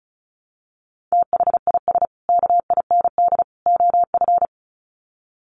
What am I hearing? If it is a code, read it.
Morse code, heard as a keyed tone, about 34 wpm, 700 Hz. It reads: THIS KIND OF
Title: Sets of 3 Words 35wpm